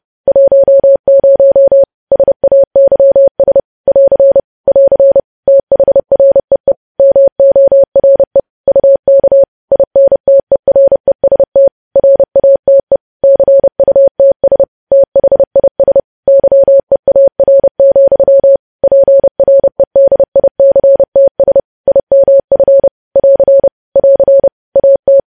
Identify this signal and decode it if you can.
Morse code, heard as a keyed tone, about 30 wpm, 550 Hz. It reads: 10 SAYS <AR> <AR> THREE MORE UK INTEREST RATE CUTS THIS YEAR, PREDICTS IMF <AR> <AR> AT
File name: podcast / News headlines-1x-30x30